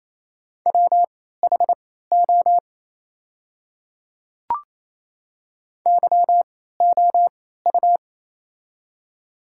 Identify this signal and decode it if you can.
Morse code, heard as a keyed tone, about 27 wpm, spaced as for 16 wpm, 690 Hz. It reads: WHO YOU
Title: 3x Character Spacing - Top 100 Words - 28wpm